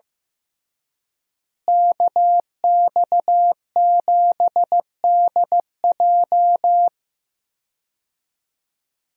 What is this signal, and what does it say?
Morse code, heard as a keyed tone, about 15 wpm, 700 Hz. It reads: KX7DJ
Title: USA Call Signs - 5 Characters - Repeated 3x - 15wpm